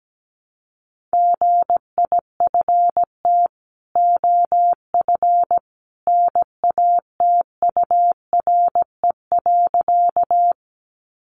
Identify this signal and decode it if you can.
Morse code, heard as a keyed tone, about 17 wpm, 700 Hz. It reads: GIFT OF NATURE.
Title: Sentences from Top 900 Words 17wpm